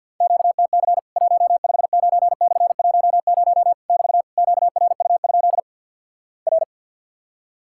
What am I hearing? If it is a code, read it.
Morse code, heard as a keyed tone, about 50 wpm, 700 Hz. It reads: QTC 159/10 <BT> QRU? R
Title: Worked All Europe DX Contest - QTC Exchange - 50wpm